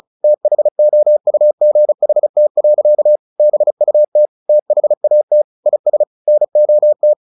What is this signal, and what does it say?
Morse code, heard as a keyed tone, about 34 wpm, 600 Hz. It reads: THOUGHT. BUT THAT IS NOT
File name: the old man and the sea 2 35wpm